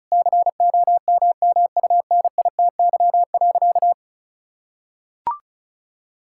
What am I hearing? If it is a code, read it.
Morse code, heard as a keyed tone, about 36 wpm, 700 Hz. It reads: COMMUNITY.
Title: Sentences from Top 1000 Words - Review Part 5 - 35wpm